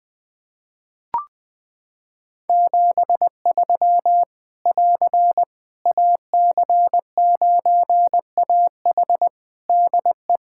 Morse code, heard as a keyed tone, about 20 wpm, 700 Hz. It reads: 73 <AR> AC9AH DE
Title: Practice QSO - Exchange 4 - Ending - 20wpm